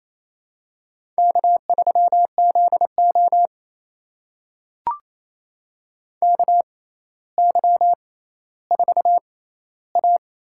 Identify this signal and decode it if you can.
Morse code, heard as a keyed tone, about 28 wpm, spaced as for 14 wpm, 700 Hz. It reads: K3ZO KY4A
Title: USA Call Signs - 4 Characters - 6x Character Spacing - 28wpm